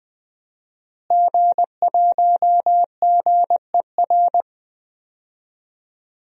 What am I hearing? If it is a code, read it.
Morse code, heard as a keyed tone, about 20 wpm, 700 Hz. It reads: G1GER
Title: DXCC-100 Call Signs - 5 Characters - Repeated 3x - 20wpm